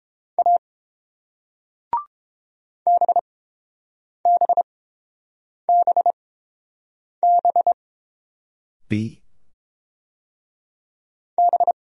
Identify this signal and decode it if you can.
Morse code, heard as a keyed tone, about 30 wpm, spaced as for 7 wpm, 700 Hz. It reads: A BBBB B